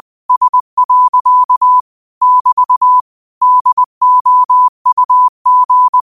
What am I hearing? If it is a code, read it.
Morse code, heard as a keyed tone, about 20 wpm, 1000 Hz. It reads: S. <BT> DOUG
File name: qotdfiles / 20wpm